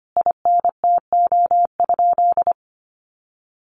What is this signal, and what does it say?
Morse code, heard as a keyed tone, about 25 wpm, 700 Hz. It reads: INTO?
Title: Sentences from Top 600 Words 25wpm